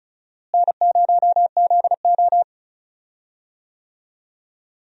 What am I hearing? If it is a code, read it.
Morse code, heard as a keyed tone, about 34 wpm, 700 Hz. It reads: N0ZO
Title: USA Call Signs - 4 Characters - Repeated 3x - 35wpm